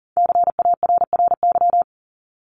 Morse code, heard as a keyed tone, about 40 wpm, 700 Hz. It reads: CARRY
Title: Top 300 Words 40wpm